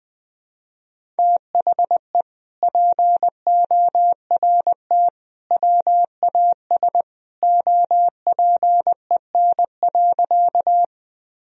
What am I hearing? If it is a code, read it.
Morse code, heard as a keyed tone, about 21 wpm, 700 Hz. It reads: THE PORT WAS OPEN.
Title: Sentences from Top 1000 Words - Review Part 1 - 20wpm